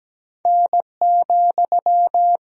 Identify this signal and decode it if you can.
Morse code, heard as a keyed tone, about 17 wpm, 700 Hz. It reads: N,
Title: Sentences from Top 300 Words and Abbreviations 17wpm